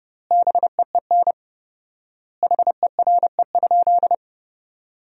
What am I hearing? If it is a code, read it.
Morse code, heard as a keyed tone, about 32 wpm, 700 Hz. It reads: BEEN HERE?
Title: 4x Word-Spacing - Sentences from Top 200 Words - 30wpm